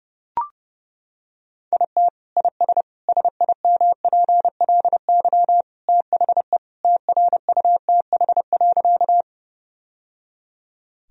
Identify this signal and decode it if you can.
Morse code, heard as a keyed tone, about 32 wpm, 700 Hz. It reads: IT IS SIMPLY THE TRUTH.
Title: Sentences from Top 1000 Words - Review Part 4 - 30wpm